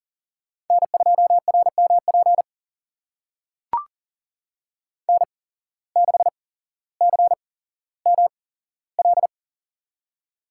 Morse code, heard as a keyed tone, about 39 wpm, spaced as for 17 wpm, 700 Hz. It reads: N2RMP N6CML
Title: USA Call Signs - 8x Character Spacing - 40wpm